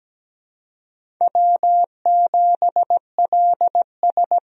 Morse code, heard as a keyed tone, about 17 wpm, 700 Hz. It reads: W7LS
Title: USA Call Signs - 4 Characters - 17wpm